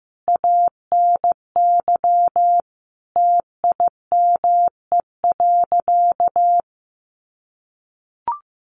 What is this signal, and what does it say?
Morse code, heard as a keyed tone, about 15 wpm, 700 Hz. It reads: ANY TIME.
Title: Sentences from Top 1100 Words 15wpm